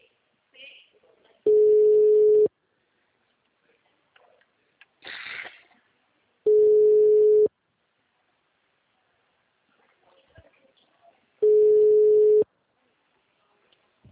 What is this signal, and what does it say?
Ringback tone (Europe, 425 Hz), with speech.
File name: out- -2023- - - .70948